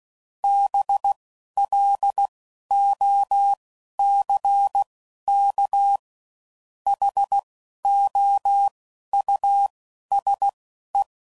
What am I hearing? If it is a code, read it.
Morse code, heard as a keyed tone, about 16 wpm, 800 Hz. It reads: B L O C K H O U S E
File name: Suffixes / General / Houses